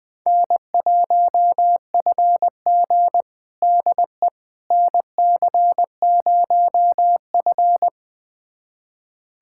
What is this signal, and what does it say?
Morse code, heard as a keyed tone, about 20 wpm, 700 Hz. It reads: N1FG DE NC0F
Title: USA Callsign DE Callsign - 4 Characters - 20wpm